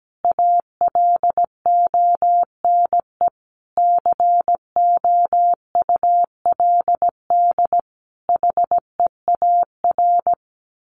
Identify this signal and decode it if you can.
Morse code, heard as a keyed tone, about 17 wpm, 700 Hz. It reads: ALONE COULD HEAR
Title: Sentences from Top 1000 Words 17wpm